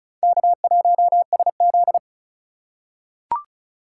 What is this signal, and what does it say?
Morse code, heard as a keyed tone, about 36 wpm, 700 Hz. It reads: K1SZ
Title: USA Call Signs - 4 Characters - 3x Character Spacing - 35wpm